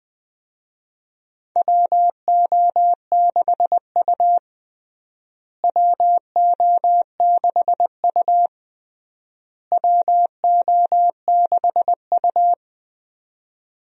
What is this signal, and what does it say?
Morse code, heard as a keyed tone, about 20 wpm, 700 Hz. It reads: WO6U WO6U WO6U
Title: USA Call Signs - 4 Characters - Repeated 3x - 20wpm